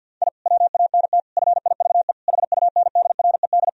Morse code, heard as a keyed tone, about 51 wpm, 700 Hz. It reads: I WANT FIVE HUNDRED
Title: Sentences from Top 1000 Words - Review Part 2 - 50wpm